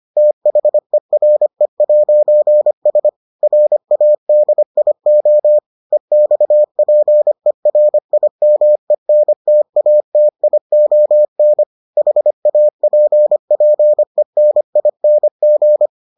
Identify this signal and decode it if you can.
Morse code, heard as a keyed tone, about 25 wpm, 600 Hz. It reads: THERE'S RADIO EXPERIMENTATION HAPPENING